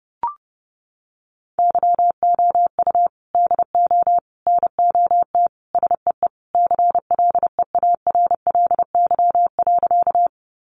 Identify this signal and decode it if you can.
Morse code, heard as a keyed tone, about 30 wpm, 700 Hz. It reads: YOU DO NOT SEE CLEARLY.
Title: Sentences from Top 1300 Words 30wpm